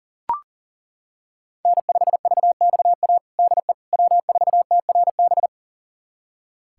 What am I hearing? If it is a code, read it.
Morse code, heard as a keyed tone, about 37 wpm, 700 Hz. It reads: N5VXA DE W4TRB
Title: USA Callsign DE Callsign - 5 Characters - 40wpm